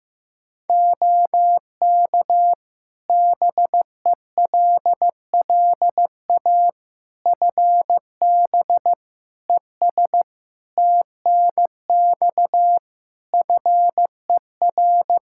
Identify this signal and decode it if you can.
Morse code, heard as a keyed tone, about 15 wpm, 700 Hz. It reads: OK BELLA FB ES TNX FER
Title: K1USN SST QSO Practice 15wpm